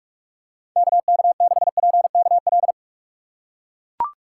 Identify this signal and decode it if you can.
Morse code, heard as a keyed tone, about 44 wpm, 700 Hz. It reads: KK6PKL
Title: USA Call Signs - 6 Characters - 6x Character Spacing - 45wpm